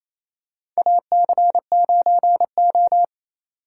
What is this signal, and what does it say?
Morse code, heard as a keyed tone, about 28 wpm, 700 Hz. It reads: AC9O
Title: USA Call Signs - 4 Characters - 4x Character Spacing - 28wpm